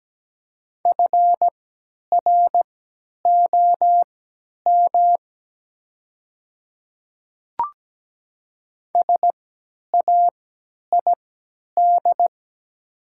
Morse code, heard as a keyed tone, about 17 wpm, spaced as for 10 wpm, 700 Hz. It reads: FROM SAID
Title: 3x Character Spacing - Top 300 Words - 17wpm